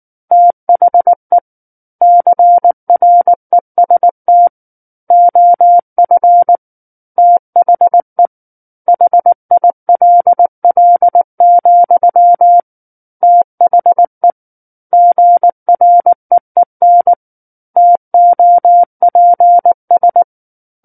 Morse code, heard as a keyed tone, about 19 wpm, 700 Hz. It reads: THE CREST OF THE HILL, THE GREEN TOPS